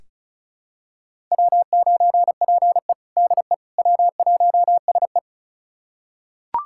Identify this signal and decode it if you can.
Morse code, heard as a keyed tone, about 36 wpm, 700 Hz. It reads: W9PE DE W1SE
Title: USA Callsign DE Callsign - 4 Characters - 35wpm